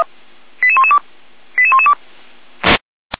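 Five-tone selective call (ZVEI), heard as quick runs of five tones, tones of about 70 ms. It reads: 88112 88112
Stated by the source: ZVEI five-tone selective call code 88537, received by SDR.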